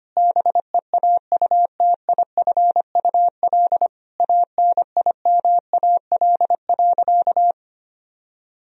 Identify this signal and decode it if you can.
Morse code, heard as a keyed tone, about 25 wpm, 700 Hz. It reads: BEAUTIFUL ANIMAL.
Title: Sentences from Top 1000 Words - Review Part 1 - 25wpm